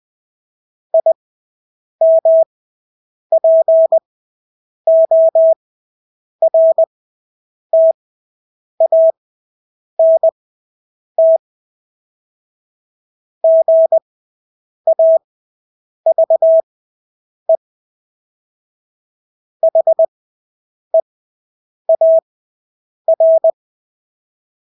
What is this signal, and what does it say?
Morse code, heard as a keyed tone, about 20 wpm, spaced as for 8 wpm, 650 Hz. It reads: IMPORTANT GAVE HEAR
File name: Engelske ord 20/8 wpm